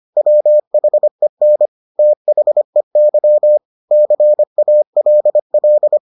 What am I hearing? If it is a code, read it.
Morse code, heard as a keyed tone, about 25 wpm, 600 Hz. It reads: WHEN THEY CALL